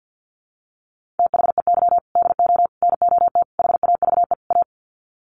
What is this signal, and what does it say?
Morse code, heard as a keyed tone, about 50 wpm, 700 Hz. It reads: THEY DO NOT HAVE A
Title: Sets of 5 Words 50wpm